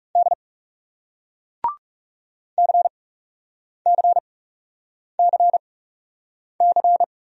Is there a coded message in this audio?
Yes: Morse code from 0.2 to 7.0 s, heard as a keyed tone, about 40 wpm, spaced as for 8 wpm, 700 Hz. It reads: D CCCC